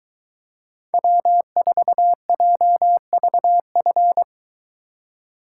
Morse code, heard as a keyed tone, about 23 wpm, 700 Hz. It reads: W4JVF